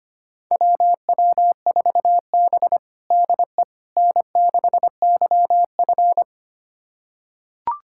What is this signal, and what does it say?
Morse code, heard as a keyed tone, about 25 wpm, 700 Hz. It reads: WW4B DE N6YF
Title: USA Callsign DE Callsign - 4 Characters - 25wpm